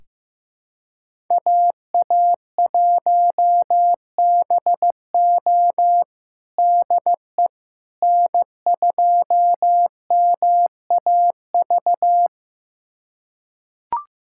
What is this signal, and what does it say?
Morse code, heard as a keyed tone, about 15 wpm, 700 Hz. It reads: AA1BO DE N2MAV
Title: USA Callsign DE Callsign - 5 Characters - 15wpm